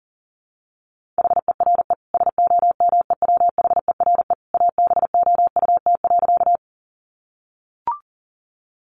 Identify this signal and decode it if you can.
Morse code, heard as a keyed tone, about 40 wpm, 700 Hz. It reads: HERE SOMEWHERE ABOUT.
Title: Sentences from Top 1400 Words 40wpm